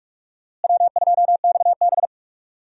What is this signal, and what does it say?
Morse code, heard as a keyed tone, about 46 wpm, 700 Hz. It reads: W2XB
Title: USA Call Signs - 4 Characters - Repeated 3x - 45wpm